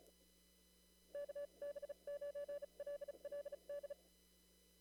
Morse code, heard as a keyed tone, about 36 wpm, spaced as for 30 wpm, 570 Hz. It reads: KB9LLD
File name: CW / kt5j wpx cw-op-k5pi